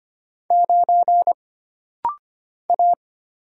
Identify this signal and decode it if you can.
Morse code, heard as a keyed tone, about 25 wpm, 700 Hz. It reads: 9 A
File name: Single Character (Letter:Number:Punctuation:Prosign) - Rapid-Fire - 25wpm